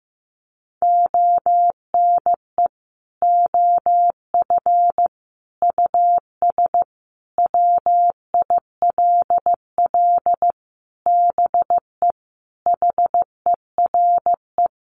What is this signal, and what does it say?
Morse code, heard as a keyed tone, about 15 wpm, 700 Hz. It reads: ONE OF US WILL BE HERE
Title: Sentences from Top 300 Words 15wpm